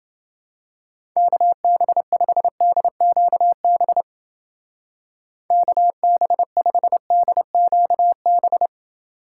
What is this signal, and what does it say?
Morse code, heard as a keyed tone, about 29 wpm, 700 Hz. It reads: KB5DQB KB5DQB